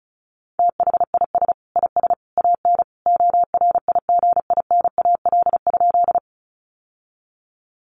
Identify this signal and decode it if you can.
Morse code, heard as a keyed tone, about 34 wpm, 700 Hz. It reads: THIS IS AN ORIGINAL?
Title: Sentences from Top 900 Words 35wpm